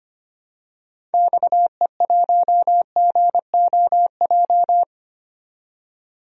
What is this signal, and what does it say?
Morse code, heard as a keyed tone, about 25 wpm, 700 Hz. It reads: XE1GOJ
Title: DXCC-100 Call Signs - 6 Characters - 25wpm